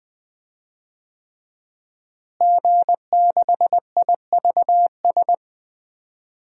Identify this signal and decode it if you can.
Morse code, heard as a keyed tone, about 21 wpm, 700 Hz. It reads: G6IVS
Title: DXCC-100 Call Signs - 5 Characters - Repeated 3x - 20wpm